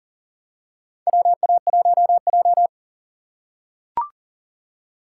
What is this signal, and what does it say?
Morse code, heard as a keyed tone, about 40 wpm, 700 Hz. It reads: WA1J
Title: USA Call Signs - 4x Character Spacing - 40wpm